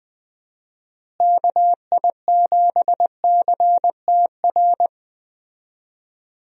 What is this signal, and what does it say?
Morse code, heard as a keyed tone, about 20 wpm, 700 Hz. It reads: KI7CTR
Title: USA Call Signs - 6 Characters - Repeated 2x - 20wpm